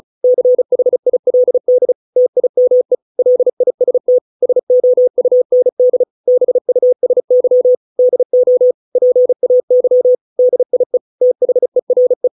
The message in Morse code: CHILD TIME LIST SOUND BUSY DO PAY DIE THERE

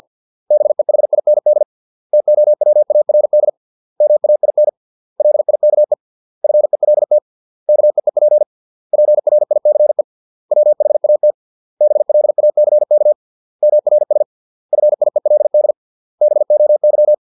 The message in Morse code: BEHIND TOWARD MAIN LIKE FELT KEEP PRICE WHAT BLACK MRS FIELD BOY

